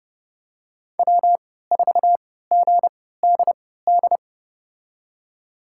Morse code, heard as a keyed tone, about 30 wpm, spaced as for 17 wpm, 700 Hz. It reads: W4GDD
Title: USA Call Signs - 5 Characters - 3x Character Spacing - 30wpm